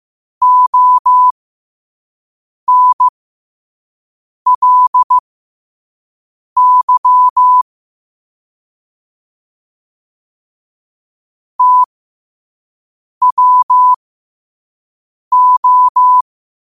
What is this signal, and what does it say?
Morse code, heard as a keyed tone, about 15 wpm, spaced as for 5 wpm, 1000 Hz. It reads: ONLY TWO
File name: qotdfiles / 5wpm